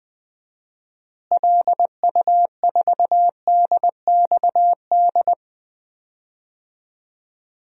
Morse code, heard as a keyed tone, about 20 wpm, 700 Hz. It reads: LU4DXD
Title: DXCC-100 Call Signs - 6 Characters - Repeated 3x - 20wpm